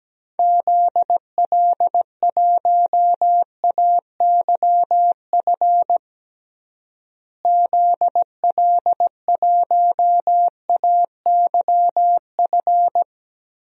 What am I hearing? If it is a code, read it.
Morse code, heard as a keyed tone, about 17 wpm, 700 Hz. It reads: ZL1AYF ZL1AYF
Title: DXCC-100 Call Signs - 6 Characters - Repeated 3x - 17wpm